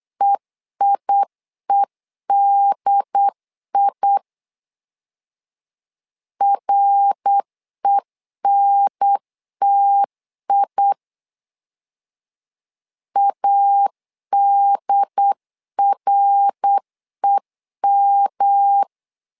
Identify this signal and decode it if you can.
Morse code, heard as a keyed tone, about 8 wpm, 800 Hz. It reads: EIEDI RENTI ADREM
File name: LEZIONI / 3 MWRO / 3 MWR 8wpm 2444